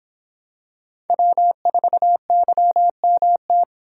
Morse code, heard as a keyed tone, about 26 wpm, 700 Hz. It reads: W4YMT